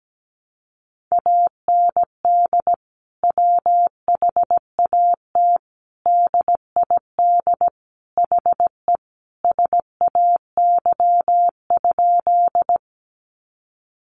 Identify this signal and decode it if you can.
Morse code, heard as a keyed tone, about 17 wpm, 700 Hz. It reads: AND WHAT DID HE SAY?
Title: Sentences from Top 200 Words 17wpm